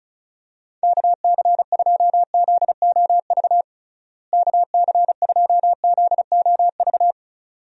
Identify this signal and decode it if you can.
Morse code, heard as a keyed tone, about 36 wpm, 700 Hz. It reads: KC2ZOV KC2ZOV
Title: USA Call Signs - 6 Characters - Repeated 2x - 35wpm